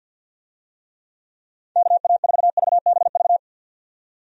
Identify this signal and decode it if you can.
Morse code, heard as a keyed tone, about 47 wpm, 700 Hz. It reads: KA4VBV
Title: USA Call Signs - 6 Characters - 50wpm